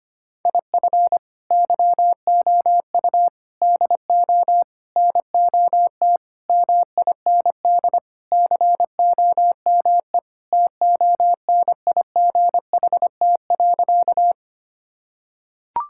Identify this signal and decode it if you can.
Morse code, heard as a keyed tone, about 25 wpm, 700 Hz. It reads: IF YOU DO NOT MIND COME TONIGHT.
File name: practice / 25wpm / Sentences with 4-Word Phrases - 25wpm